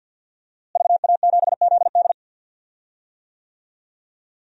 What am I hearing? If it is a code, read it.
Morse code, heard as a keyed tone, about 47 wpm, 700 Hz. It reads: VA7ZD
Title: DXCC-100 Call Signs - 5 Characters - Repeated 3x - 50wpm